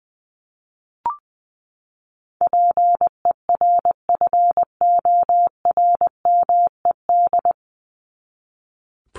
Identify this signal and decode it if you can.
Morse code, heard as a keyed tone, about 20 wpm, 700 Hz. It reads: PERFORMED
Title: ED Suffix - Top 100 Words with Base Words 20wpm